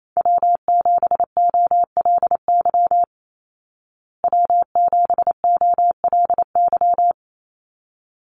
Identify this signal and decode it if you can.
Morse code, heard as a keyed tone, about 28 wpm, 700 Hz. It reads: W7OLY W7OLY
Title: Call Signs Repeated 3x - 28wpm